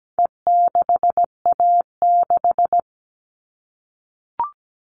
Morse code, heard as a keyed tone, about 17 wpm, 700 Hz. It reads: E6A6
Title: All Cut Numbers - 4-Digit 17wpm